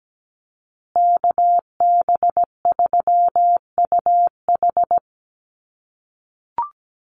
Morse code, heard as a keyed tone, about 17 wpm, 700 Hz. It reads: KB3UH
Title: USA Call Signs Repeated 3x 17wpm